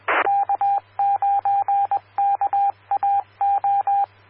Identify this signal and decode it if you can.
Morse code, heard as a keyed tone, about 21 wpm, 800 Hz, with radio noise. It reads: K9KAO
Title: Claremont, CA Police, Amateur Radio